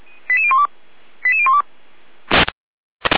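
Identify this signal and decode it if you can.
Five-tone selective call (ZVEI), heard as quick runs of five tones, tones of about 70 ms. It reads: 88012 88012
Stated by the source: ZVEI five-tone selective call 88104, SDR recording.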